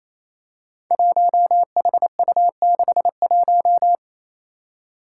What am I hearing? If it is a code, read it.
Morse code, heard as a keyed tone, about 29 wpm, 700 Hz. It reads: 1HU61
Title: Five Challenging Letter-Number Combinations 28wpm